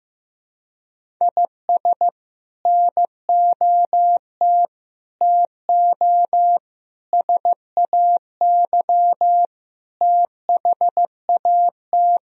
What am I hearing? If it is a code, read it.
Morse code, heard as a keyed tone, about 15 wpm, 700 Hz. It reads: IS NOT TO SAY THAT
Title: Speed-Racing - Sets of 5 Words 15wpm